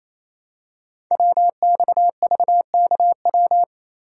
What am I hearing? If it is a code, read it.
Morse code, heard as a keyed tone, about 28 wpm, 700 Hz. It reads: WXVKW
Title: Code Groups - 5 Letters - 28wpm